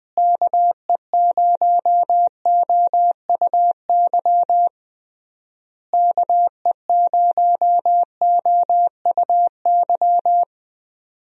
Morse code, heard as a keyed tone, about 20 wpm, 700 Hz. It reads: KE0OUY KE0OUY
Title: USA Call Signs - 6 Characters - Repeated 3x - 20wpm